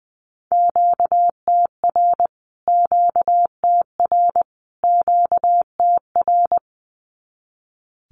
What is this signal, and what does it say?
Morse code, heard as a keyed tone, about 20 wpm, 700 Hz. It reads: QTR QTR QTR
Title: Q-Codes Repeated 3x 20wpm